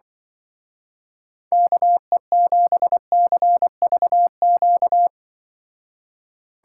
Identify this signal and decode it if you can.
Morse code, heard as a keyed tone, about 25 wpm, 700 Hz. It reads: KE7CVQ
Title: Speed Racing - USA 2x3 Call Signs 20wpm